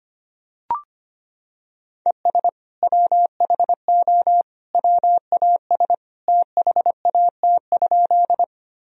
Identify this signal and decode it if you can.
Morse code, heard as a keyed tone, about 25 wpm, 700 Hz. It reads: ES WHO WAS THAT?
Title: Sentences from Top 100 Words and Abbreviations 25wpm